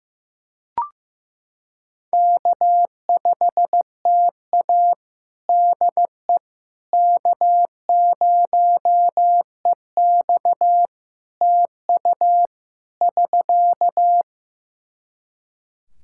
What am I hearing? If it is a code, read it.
Morse code, heard as a keyed tone, about 15 wpm, 700 Hz. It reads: K5TA DE K0EX TU <SK>
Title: Practice QSO - Exchange 4 - Ending - 15wpm